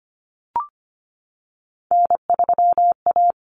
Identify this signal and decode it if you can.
Morse code, heard as a keyed tone, about 25 wpm, 700 Hz. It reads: N3A
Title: Common Cut Numbers - 3-Digit 25wpm